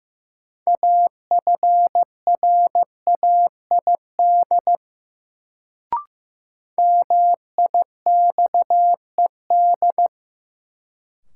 Morse code, heard as a keyed tone, about 15 wpm, 700 Hz. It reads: AFRAID MIXED
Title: Sentences from Top 1500 Words - New Word Drill Alt - 15wpm